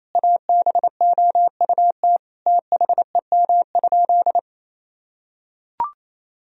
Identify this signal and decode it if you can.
Morse code, heard as a keyed tone, about 28 wpm, 700 Hz. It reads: ABOUT THEM?
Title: Sentences from Top 1000 Words - Review Part 3 - 28wpm